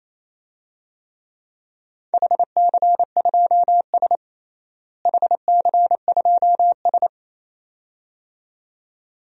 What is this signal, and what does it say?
Morse code, heard as a keyed tone, about 28 wpm, 700 Hz. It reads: HC2S HC2S
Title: DXCC-100 Call Signs - 4 Characters - Repeated 2x - 28wpm